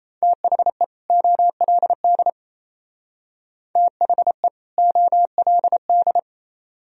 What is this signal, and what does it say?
Morse code, heard as a keyed tone, about 36 wpm, 690 Hz. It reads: THE OLD THE OLD